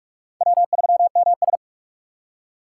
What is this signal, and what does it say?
Morse code, heard as a keyed tone, about 46 wpm, 700 Hz. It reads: W3MS
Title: K1USN SST QSO Practice - 45wpm